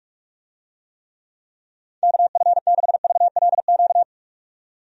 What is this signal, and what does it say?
Morse code, heard as a keyed tone, about 46 wpm, 700 Hz. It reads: KF6VLQ
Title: USA Call Signs - 6 Characters - 45wpm